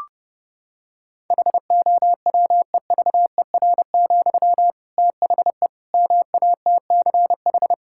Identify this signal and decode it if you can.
Morse code, heard as a keyed tone, about 30 wpm, 700 Hz. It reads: HOWEVER, THE MATCH
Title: Sentences from Top 1600 Words 30wpm